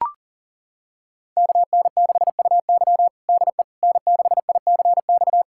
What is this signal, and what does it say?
Morse code, heard as a keyed tone, about 40 wpm, 700 Hz. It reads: KN6UY DE N6ICX
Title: USA Callsign DE Callsign - 5 Characters - 40wpm